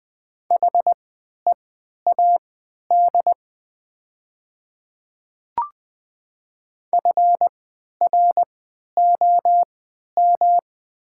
Morse code, heard as a keyed tone, about 20 wpm, spaced as for 11 wpm, 700 Hz. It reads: HEAD FROM
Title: 3x Character Spacing - Top 300 Words - 20wpm